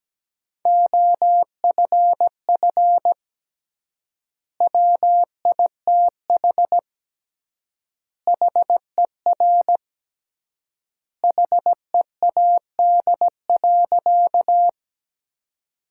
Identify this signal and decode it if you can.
Morse code, heard as a keyed tone, about 17 wpm, 700 Hz. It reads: OFF WITH HER HEAD.
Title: 3x Word-Spacing - Sentences from Top 300 Words - 17wpm